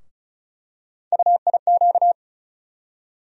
Morse code, heard as a keyed tone, about 34 wpm, 700 Hz. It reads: UIQ
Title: USA Call Sign Postfixes - 3 Letters 35wpm